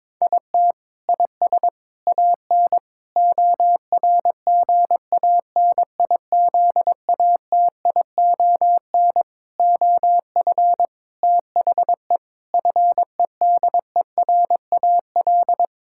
Morse code, heard as a keyed tone, about 22 wpm, 700 Hz. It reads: IT IS AN ORGANIZATION OF THE FEDERAL
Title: Sentences from Top 1000 Words - Review Part 6 - 22wpm